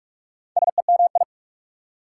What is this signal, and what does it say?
Morse code, heard as a keyed tone, about 44 wpm, 690 Hz. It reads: SEMI
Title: Prefixes and Suffixes 45wpm